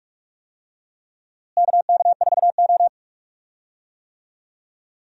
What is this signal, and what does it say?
Morse code, heard as a keyed tone, about 43 wpm, 700 Hz. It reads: KK4O
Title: USA Call Signs - 4 Characters - Repeated 2x - 45wpm